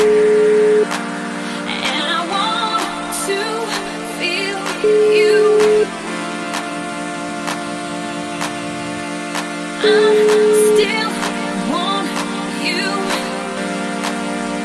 Ringback tone (Europe, 425 Hz), with noise.